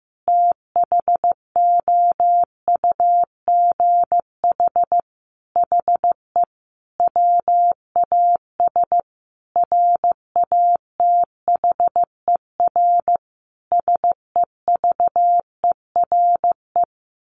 Morse code, heard as a keyed tone, about 15 wpm, 700 Hz. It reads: THOUGH HE WAS RATHER SEVERE
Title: Sentences from Top 1400 Words 15wpm